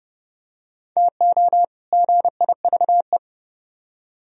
Morse code, heard as a keyed tone, about 29 wpm, 700 Hz. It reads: TO GIVE